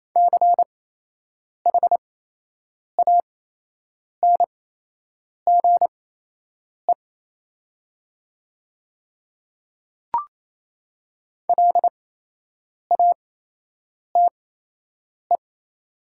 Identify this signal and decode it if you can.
Morse code, heard as a keyed tone, about 28 wpm, spaced as for 8 wpm, 700 Hz. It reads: CHANGE LATE
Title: 8x Character Spacing - Top 500 Words - 28wpm